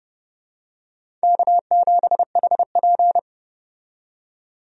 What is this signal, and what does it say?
Morse code, heard as a keyed tone, about 30 wpm, 700 Hz. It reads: K7HP
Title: USA Call Signs - 4 Characters - 30wpm